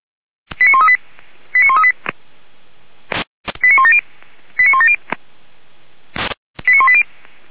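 Five-tone selective call (ZVEI), heard as quick runs of five tones, tones of about 70 ms. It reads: 89138 89138 89170 89170 89180
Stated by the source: ZVEI five-tone call 89132 received by SDR, from feuerwehr-orlamuende.org (fire brigade Orlamünde).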